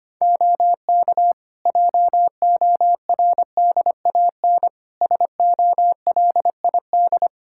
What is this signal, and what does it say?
Morse code, heard as a keyed tone, about 25 wpm, 700 Hz. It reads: OK JORDAN SOLID
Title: Practice QSO - Exchange 3 - More Info - 25wpm